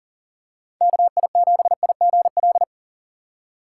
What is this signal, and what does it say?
Morse code, heard as a keyed tone, about 40 wpm, 700 Hz. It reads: KI7IGL
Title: USA Call Signs - 6 Characters - 6x Character Spacing - 40wpm